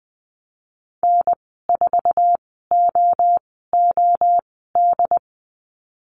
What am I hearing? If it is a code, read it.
Morse code, heard as a keyed tone, about 20 wpm, spaced as for 14 wpm, 700 Hz. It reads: N4OOD
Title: 069 Lesson 12g - Letter d - Call Signs 20wpm